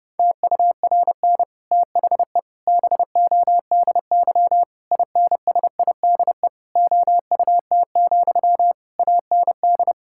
Morse code, heard as a keyed tone, about 30 wpm, 700 Hz. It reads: TURN THE BODY INSIDE OUT, AND
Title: Sentences from Top 1000 Words - Review Part 6 - 30wpm